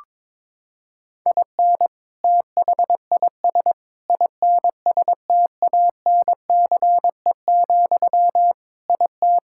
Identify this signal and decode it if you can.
Morse code, heard as a keyed tone, about 22 wpm, 700 Hz. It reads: IN THIS INSTANCE, IT
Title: Sentences from Top 1000 Words - Review Part 5 - 22wpm